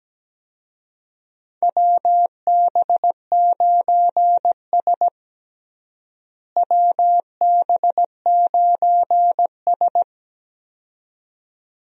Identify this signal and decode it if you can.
Morse code, heard as a keyed tone, about 17 wpm, 700 Hz. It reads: WB9S WB9S
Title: USA Call Signs - 4 Characters - Repeated 2x - 17wpm